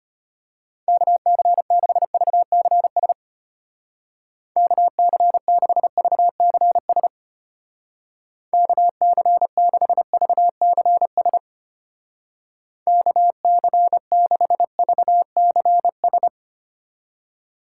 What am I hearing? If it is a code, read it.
Morse code, heard as a keyed tone, about 32 wpm, 700 Hz. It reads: KC6VCS KC6VCS KC6VCS KC6VCS